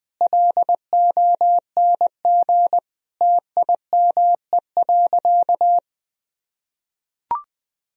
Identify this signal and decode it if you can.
Morse code, heard as a keyed tone, about 20 wpm, 700 Hz. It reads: LONG TIME.
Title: Sentences from Top 1000 Words - Review Part 4 - 20wpm